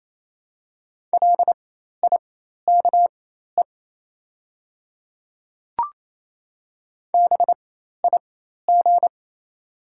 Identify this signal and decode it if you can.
Morse code, heard as a keyed tone, about 28 wpm, spaced as for 13 wpm, 700 Hz. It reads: LIKE BIG
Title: 4x Character Spacing - Top 500 Words - 28wpm